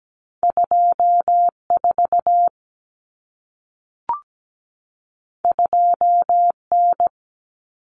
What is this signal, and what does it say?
Morse code, heard as a keyed tone, about 17 wpm, 700 Hz. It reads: 24 2N
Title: Common Cut Numbers - 2-Digit 17wpm